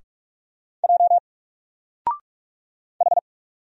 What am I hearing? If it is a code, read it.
Morse code, heard as a keyed tone, about 44 wpm, 700 Hz. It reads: J H
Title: Single Character (Letter:Number:Punctuation:Prosign) 45wpm